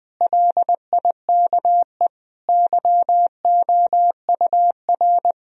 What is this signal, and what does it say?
Morse code, heard as a keyed tone, about 21 wpm, 700 Hz. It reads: LIKE YOUR
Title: Sentences from Top 1000 Words - Review Part 4 - 20wpm